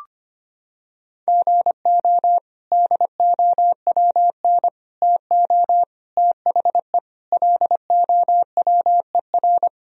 Morse code, heard as a keyed tone, about 25 wpm, 700 Hz. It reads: GO DOWN TO THE LOWER
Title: Sentences from Top 1000 Words - Review Part 4 - 25wpm